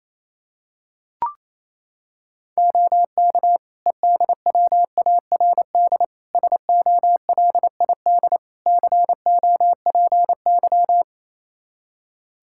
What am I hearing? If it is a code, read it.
Morse code, heard as a keyed tone, about 28 wpm, 700 Hz. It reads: OK EDWARD SOLID COPY
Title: Practice QSO - Exchange 3 - More Info - 28wpm